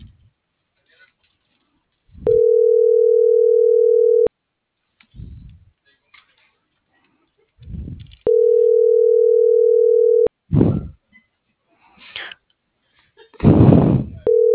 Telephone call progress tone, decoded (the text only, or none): ringback tone